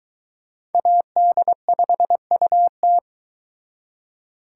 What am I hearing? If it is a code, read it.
Morse code, heard as a keyed tone, about 22 wpm, 700 Hz. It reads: AD5UT